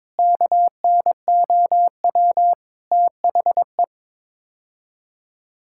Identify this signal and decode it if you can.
Morse code, heard as a keyed tone, about 22 wpm, 700 Hz. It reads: KNOW THE
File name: practice / 22wpm / Build A Phrase - 5-Word Phrases - 22wpm